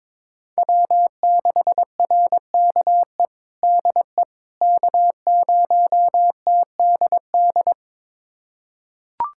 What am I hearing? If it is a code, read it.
Morse code, heard as a keyed tone, about 22 wpm, 700 Hz. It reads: W6RKE DE K0TDD
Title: USA Callsign DE Callsign - 5 Characters - 22wpm